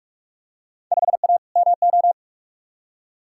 Morse code, heard as a keyed tone, about 43 wpm, 700 Hz. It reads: 5A MO
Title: Field Day Exchanges Encore - 45wpm